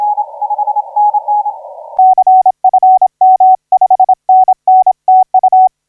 Morse code, heard as a keyed tone, about 26 wpm, 750 Hz. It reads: CFM 5NN TU
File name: 9G5AA 1712UTC 12m